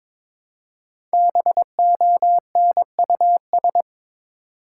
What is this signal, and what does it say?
Morse code, heard as a keyed tone, about 22 wpm, 700 Hz. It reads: BONUS
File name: 5 Letter Words - First 3 Letters Given - 22wpm